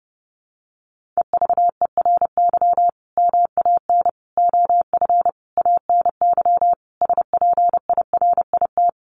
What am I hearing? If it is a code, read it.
Morse code, heard as a keyed tone, about 30 wpm, 700 Hz. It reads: EVERY MAN OF ANY SPIRIT
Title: Sentences from Top 1400 Words 30wpm